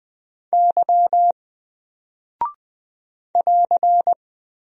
Morse code, heard as a keyed tone, about 20 wpm, 700 Hz. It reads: Y <AR>
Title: Single Character (Letter:Number:Punctuation:Prosign) Alt 20wpm